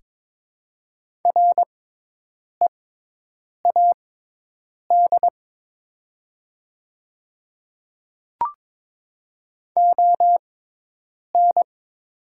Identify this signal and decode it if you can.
Morse code, heard as a keyed tone, about 22 wpm, spaced as for 8 wpm, 700 Hz. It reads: READ ON
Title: 6x Character Spacing - Top 500 Words - 22wpm